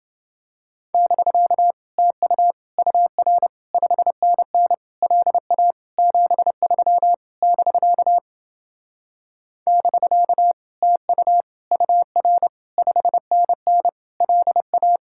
Morse code, heard as a keyed tone, about 31 wpm, 700 Hz. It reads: <BK> TU UR 5NN LA 73 <BK> <BK> TU UR 5NN LA